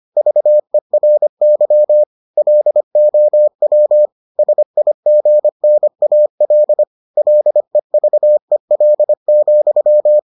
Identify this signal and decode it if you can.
Morse code, heard as a keyed tone, about 25 wpm, 600 Hz. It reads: VERY LOW SIGNAL LEVEL,